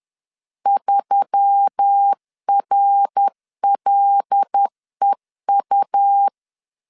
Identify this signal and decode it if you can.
Morse code, heard as a keyed tone, about 10 wpm, 800 Hz. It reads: 3RLEU
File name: LEZIONI / 8 YJ45 / 8 Y 10wpm 2615